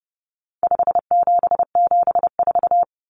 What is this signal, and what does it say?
Morse code, heard as a keyed tone, about 30 wpm, 700 Hz. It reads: 5774
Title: Common Cut Numbers - 4-Digit 30wpm